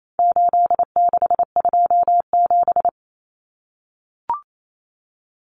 Morse code, heard as a keyed tone, about 28 wpm, 700 Hz. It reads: 8627
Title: Numbers - 4-Digit 28wpm